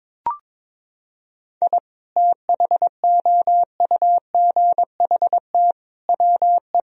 Morse code, heard as a keyed tone, about 22 wpm, 700 Hz. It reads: I THOUGHT WE
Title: Sentences from Top 1000 Words - Review Part 2 - 22wpm